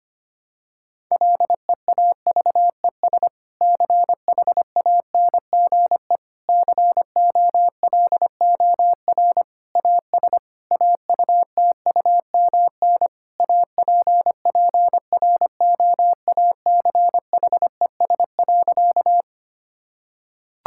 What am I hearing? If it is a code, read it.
Morse code, heard as a keyed tone, about 25 wpm, 700 Hz. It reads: LEAVES CHANGE COLOR AS AUTUMN APPROACHES.
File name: practice / 25wpm / Difficult Sentences with Hints - 25wpm